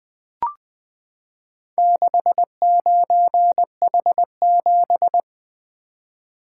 Morse code, heard as a keyed tone, about 20 wpm, 700 Hz. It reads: 69H7
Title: Four Letter-Number Combinations - 20wpm